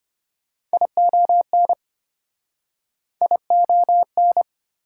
Morse code, heard as a keyed tone, about 34 wpm, 690 Hz. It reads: ION ION